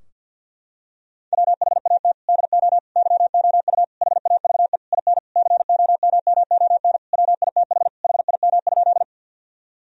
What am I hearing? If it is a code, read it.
Morse code, heard as a keyed tone, about 48 wpm, 700 Hz. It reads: WHAT DO YOU HAVE IN COMMON WITH HIM?
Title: Sentences from Top 1000 Words - Review Part 3 - 50wpm